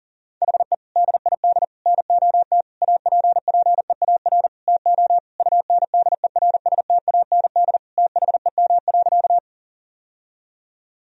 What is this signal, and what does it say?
Morse code, heard as a keyed tone, about 40 wpm, 700 Hz. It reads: HE DID NOT APPEAR TO UNDERSTAND THEM.
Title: Sentences from Top 1000 Words - Review Part 2 - 40wpm